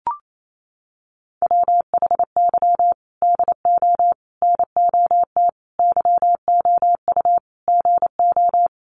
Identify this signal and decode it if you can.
Morse code, heard as a keyed tone, about 28 wpm, 700 Hz. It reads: WHY DO NOT YOU GO
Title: Sets of 5 Words Encore 28wpm